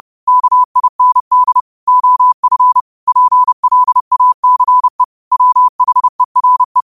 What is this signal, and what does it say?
Morse code, heard as a keyed tone, about 30 wpm, 1000 Hz. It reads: KIND OF PLACE WHERE
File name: qotdfiles / 30wpm